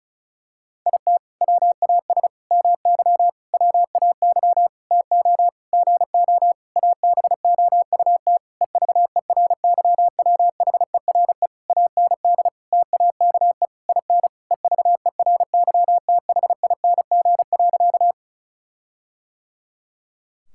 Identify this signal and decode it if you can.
Morse code, heard as a keyed tone, about 36 wpm, 700 Hz. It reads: IT WAS MY WAY TO GO ABOUT EVERYWHERE AND TAKE IN EVERYTHING.
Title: Sentences from Top 1600 Words 35wpm